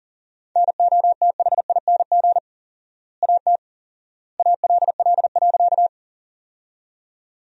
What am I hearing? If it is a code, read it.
Morse code, heard as a keyed tone, about 40 wpm, 700 Hz. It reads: NOTHING AT ALL.
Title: 4x Word-Spacing - Sentences from Top 500 Words - 40wpm